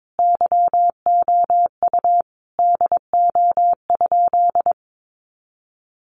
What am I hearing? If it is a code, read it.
Morse code, heard as a keyed tone, about 22 wpm, 700 Hz. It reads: YOU DO?
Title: Sentences from Top 100 Words 22wpm